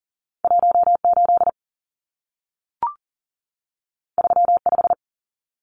Morse code, heard as a keyed tone, about 40 wpm, 700 Hz. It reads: 18 35
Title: Numbers - 2-Digit 40wpm